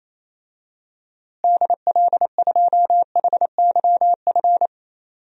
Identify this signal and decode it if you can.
Morse code, heard as a keyed tone, about 28 wpm, 700 Hz. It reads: DL2HYF
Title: DXCC-100 Call Signs - 6 Characters - Repeated 2x - 28wpm